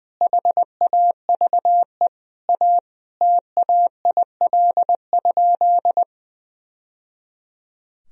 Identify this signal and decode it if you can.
Morse code, heard as a keyed tone, about 20 wpm, 700 Hz. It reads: HAVE A TAIL?
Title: Sentences from Top 1000 Words - Review Part 3 - 20wpm